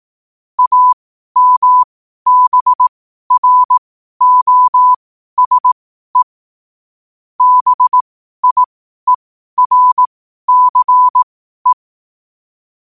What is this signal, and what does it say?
Morse code, heard as a keyed tone, about 18 wpm, spaced as for 12 wpm, 1000 Hz. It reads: AMBROSE BIERCE
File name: qotdfiles / 12wpm